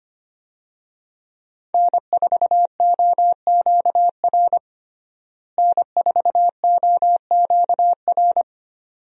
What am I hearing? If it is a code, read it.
Morse code, heard as a keyed tone, about 25 wpm, 700 Hz. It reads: N4OQR N4OQR
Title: USA Call Signs - 5 Characters - Repeated 2x - 25wpm